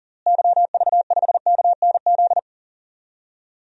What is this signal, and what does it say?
Morse code, heard as a keyed tone, about 40 wpm, 700 Hz. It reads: YV5KNZ
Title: DXCC-100 Call Signs - 6 Characters - Repeated 3x - 40wpm